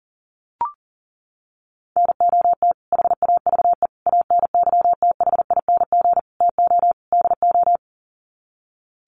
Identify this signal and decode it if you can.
Morse code, heard as a keyed tone, about 40 wpm, 700 Hz. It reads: NOT HAVE ANYTHING TO DO
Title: Sets of 5 Words Encore 40wpm